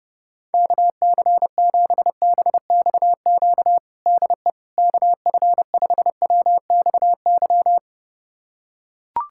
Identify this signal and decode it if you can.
Morse code, heard as a keyed tone, about 32 wpm, 700 Hz. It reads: KC7BXQ DE KF5WXY
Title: USA Callsign DE Callsign 30wpm